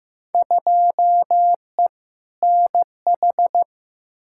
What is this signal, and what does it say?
Morse code, heard as a keyed tone, about 15 wpm, 700 Hz. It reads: 2E NH
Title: Field Day Exchanges - 15wpm